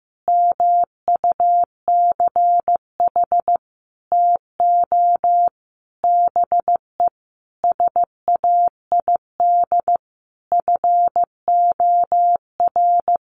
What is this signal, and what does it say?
Morse code, heard as a keyed tone, about 14 wpm, 700 Hz. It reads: MUCH TO BE SAID FOR
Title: Sentences from Top 1300 Words 15wpm